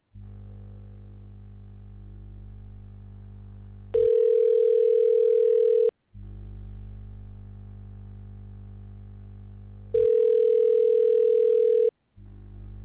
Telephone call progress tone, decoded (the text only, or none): ringback tone